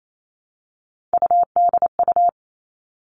Morse code, heard as a keyed tone, about 28 wpm, 700 Hz. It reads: UDU
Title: Three Challenging Letter-Number Combinations 28wpm